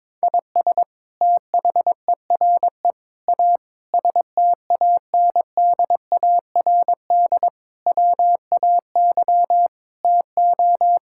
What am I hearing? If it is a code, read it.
Morse code, heard as a keyed tone, about 22 wpm, 700 Hz. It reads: IS THERE A STANDARD WAY TO
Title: Sentences from Top 1000 Words - Review Part 6 - 22wpm